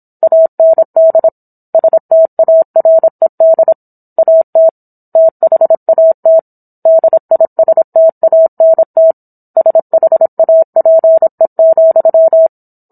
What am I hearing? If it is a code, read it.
Morse code, heard as a keyed tone, about 27 wpm, 650 Hz. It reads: AND STARED AT THAT DISTANT SHAPE,